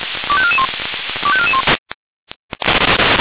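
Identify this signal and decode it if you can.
Five-tone selective call (ZVEI), heard as quick runs of five tones, tones of about 70 ms. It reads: 25441 25441
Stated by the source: ZVEI five-tone selective call received on an SDR.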